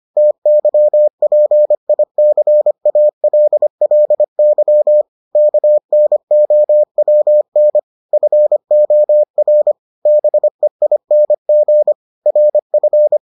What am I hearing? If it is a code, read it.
Morse code, heard as a keyed tone, about 25 wpm, 600 Hz. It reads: TYPICALLY KNOWN FOR BEING RF